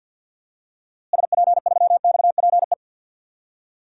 Morse code, heard as a keyed tone, about 50 wpm, 700 Hz. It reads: SP3XPE
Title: DXCC-100 Call Signs - 6 Characters - Repeated 2x - 50wpm